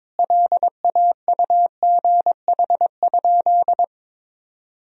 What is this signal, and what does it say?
Morse code, heard as a keyed tone, about 22 wpm, 700 Hz. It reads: LAUGH?
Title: Sentences from Top 500 Words and Abbreviations 22wpm